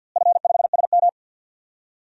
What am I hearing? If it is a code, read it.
Morse code, heard as a keyed tone, about 53 wpm, 700 Hz. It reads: F5SM
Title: DXCC-100 Call Signs - 4 Characters - Repeated 3x - 50wpm